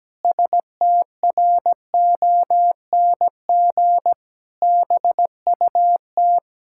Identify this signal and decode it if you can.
Morse code, heard as a keyed tone, about 17 wpm, 700 Hz. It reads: STRONG BUT
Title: Pangram Challenge v2 - 17wpm